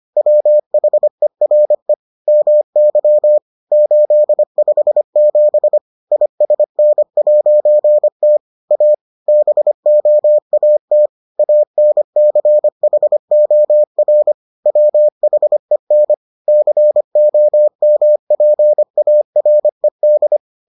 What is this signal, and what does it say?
Morse code, heard as a keyed tone, about 25 wpm, 600 Hz. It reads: WHERE MY 857 ISN'T A BOAT ANCHOR WHEN COMPARED